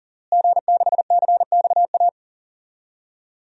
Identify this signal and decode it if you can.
Morse code, heard as a keyed tone, about 40 wpm, 700 Hz. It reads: G6CXA
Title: DXCC-100 Call Signs - 5 Characters - Repeated 2x - 40wpm